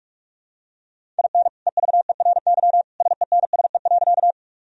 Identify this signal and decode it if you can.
Morse code, heard as a keyed tone, about 44 wpm, 700 Hz. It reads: IN EVERY SENSE.
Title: Sentences from Top 1000 Words - Review Part 4 - 45wpm